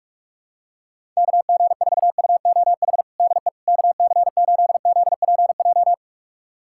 Morse code, heard as a keyed tone, about 42 wpm, 700 Hz. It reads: KG4UOH DE KC8ZPJ
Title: USA Callsign DE Callsign - 6 Characters - 45wpm